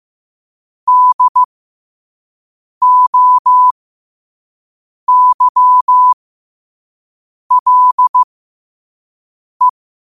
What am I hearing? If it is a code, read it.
Morse code, heard as a keyed tone, about 15 wpm, spaced as for 5 wpm, 1000 Hz. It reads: DOYLE
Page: Quotes for Wed, 13 Aug 2025 in Morse Code at 5 words per minute.